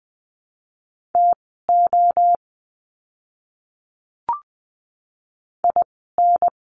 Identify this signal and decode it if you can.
Morse code, heard as a keyed tone, about 20 wpm, spaced as for 14 wpm, 700 Hz. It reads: TO IN
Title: 139 Lesson 23f - Letter f - Words 20wpm